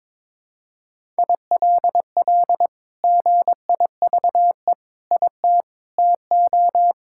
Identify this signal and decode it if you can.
Morse code, heard as a keyed tone, about 21 wpm, 700 Hz. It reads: ILL GIVE IT TO
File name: practice / 22wpm / Sentences with 2-Word Phrases - 22wpm